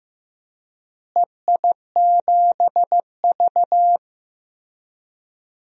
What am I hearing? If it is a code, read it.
Morse code, heard as a keyed tone, about 15 wpm, 700 Hz. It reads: EI7V
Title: DXCC-100 Call Signs - 4 Characters - Repeated 2x - 15wpm